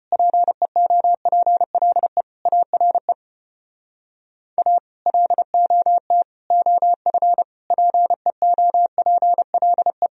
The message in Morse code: PEOPLE ARE A LOT OF PEOPLE

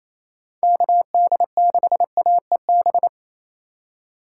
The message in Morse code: KD6AEB

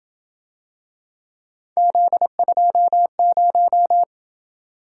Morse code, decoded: Z20

